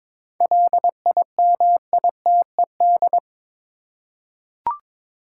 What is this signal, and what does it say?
Morse code, heard as a keyed tone, about 22 wpm, 700 Hz. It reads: LIMITED
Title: Seven Letter Words - 22wpm